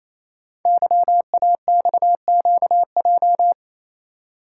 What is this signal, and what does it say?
Morse code, heard as a keyed tone, about 27 wpm, 700 Hz. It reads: YAXQJ
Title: Code Groups - 5 Letters - 28wpm